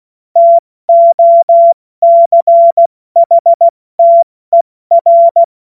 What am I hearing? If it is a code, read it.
Morse code, heard as a keyed tone, about 16 wpm, 680 Hz. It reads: TOCHTER
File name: Morsekurs / Witze / Witz 25. April 70 F80